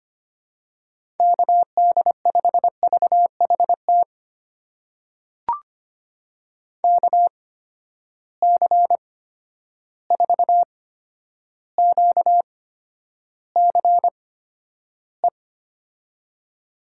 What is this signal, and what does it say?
Morse code, heard as a keyed tone, about 25 wpm, spaced as for 11 wpm, 700 Hz. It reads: KD5VHT KC4QCE
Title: USA Call Signs - 6 Characters - 8x Character Spacing - 25wpm